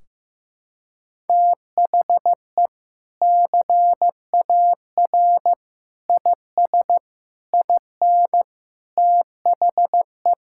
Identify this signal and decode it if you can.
Morse code, heard as a keyed tone, about 15 wpm, 700 Hz. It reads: THE CAR IS IN THE
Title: Sentences from Top 1600 Words 15wpm